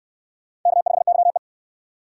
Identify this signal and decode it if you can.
Morse code, heard as a keyed tone, about 65 wpm, 700 Hz. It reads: K5YE